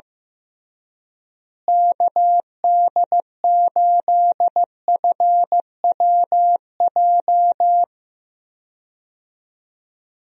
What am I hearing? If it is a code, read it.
Morse code, heard as a keyed tone, about 16 wpm, 700 Hz. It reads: KD8FWJ
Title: USA Call Signs - 6 Characters - Repeated 3x - 15wpm